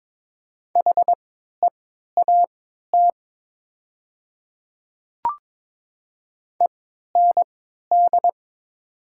Morse code, heard as a keyed tone, about 22 wpm, spaced as for 12 wpm, 700 Hz. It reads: HEAT END
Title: 3x Character Spacing - Top 500 Words - 22wpm